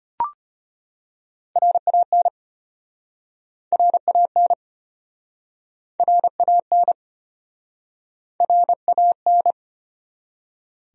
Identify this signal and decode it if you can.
Morse code, heard as a keyed tone, about 33 wpm, 700 Hz. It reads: RAN RAN RAN RAN